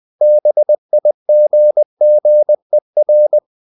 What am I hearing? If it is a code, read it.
Morse code, heard as a keyed tone, about 20 wpm, 600 Hz. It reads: BIGGER
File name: the old man and the sea / the old man and the sea part 1 20wpm